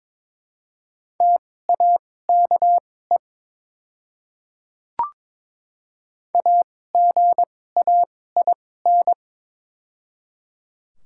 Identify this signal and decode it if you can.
Morse code, heard as a keyed tone, about 22 wpm, spaced as for 16 wpm, 700 Hz. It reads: TAKE AGAIN
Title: 2x Character Spacing - Top 300 Words - 22wpm